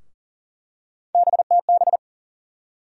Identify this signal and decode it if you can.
Morse code, heard as a keyed tone, about 40 wpm, 700 Hz. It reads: BTB
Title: USA Call Sign Postfixes - 3 Letters 40wpm